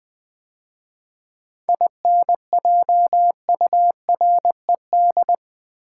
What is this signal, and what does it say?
Morse code, heard as a keyed tone, about 20 wpm, 700 Hz. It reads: INJURED
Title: 7 Letter Words - First 3 Letters Given - 20wpm